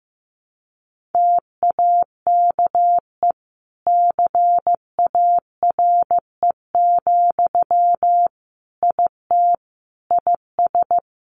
Morse code, heard as a keyed tone, about 15 wpm, 700 Hz. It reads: TAKE CARE, IT IS
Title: Sentences from Top 1100 Words 15wpm